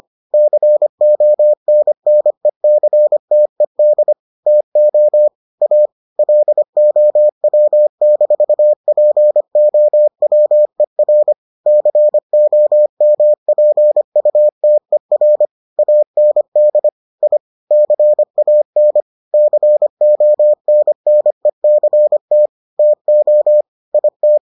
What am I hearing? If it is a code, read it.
Morse code, heard as a keyed tone, about 25 wpm, 600 Hz. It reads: CONNECTED TO A LOW-POWER COMPUTER AND I CAN CONNECT TO IT